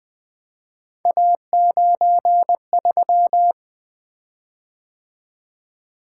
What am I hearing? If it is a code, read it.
Morse code, heard as a keyed tone, about 20 wpm, 700 Hz. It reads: A93
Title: Three Letter-Number Combinations - 20wpm